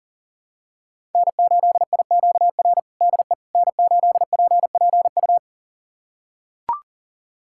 Morse code, heard as a keyed tone, about 40 wpm, 700 Hz. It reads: N8IQR DE N8PPU
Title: USA Callsign DE Callsign - 5 Characters - 40wpm